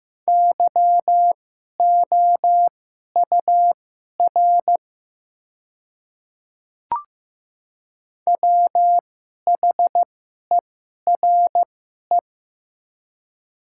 Morse code, heard as a keyed tone, about 15 wpm, spaced as for 11 wpm, 700 Hz. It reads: YOUR WHERE
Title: 2x Character Spacing - Top 200 Words - 15wpm